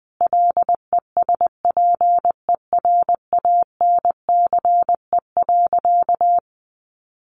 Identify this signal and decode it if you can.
Morse code, heard as a keyed tone, about 20 wpm, 700 Hz. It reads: LESPERANCE.
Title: A Sound Of Thunder 20wpm